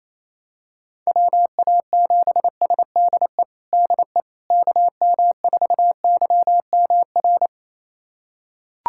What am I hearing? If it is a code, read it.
Morse code, heard as a keyed tone, about 28 wpm, 700 Hz. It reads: WA7SDE DE KM4YMR
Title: USA Callsign DE Callsign - 6 Characters - 28wpm